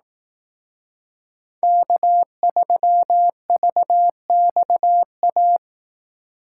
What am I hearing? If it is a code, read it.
Morse code, heard as a keyed tone, about 18 wpm, 700 Hz. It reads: K3VXA
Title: Speed Racing - USA 1x3 Call Signs 15wpm